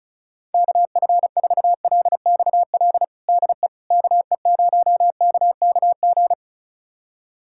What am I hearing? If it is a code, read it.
Morse code, heard as a keyed tone, about 36 wpm, 700 Hz. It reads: KF4LXL DE KE0KKG
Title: USA Callsign DE Callsign - 6 Characters - 35wpm